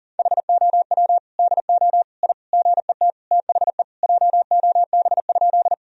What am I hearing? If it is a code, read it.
Morse code, heard as a keyed tone, about 40 wpm, 700 Hz. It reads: HOW DO I GET THE JOB?
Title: Sentences from Top 1000 Words - Review Part 3 - 40wpm